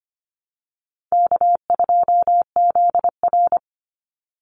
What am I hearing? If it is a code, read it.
Morse code, heard as a keyed tone, about 25 wpm, 700 Hz. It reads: K2ZR
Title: Easy Call Signs - 25wpm